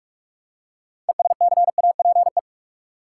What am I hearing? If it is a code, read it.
Morse code, heard as a keyed tone, about 46 wpm, 700 Hz. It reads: ESCAPE
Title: Six Letter Words - 45wpm